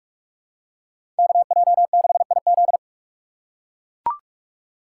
Morse code, heard as a keyed tone, about 42 wpm, 700 Hz. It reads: KJ6IZ
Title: USA Call Signs - 3x Character Spacing - 45wpm